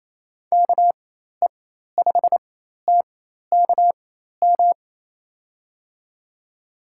Morse code, heard as a keyed tone, about 28 wpm, spaced as for 13 wpm, 700 Hz. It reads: KE5TKM
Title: USA Call Signs - 6 Characters - 4x Character Spacing - 28wpm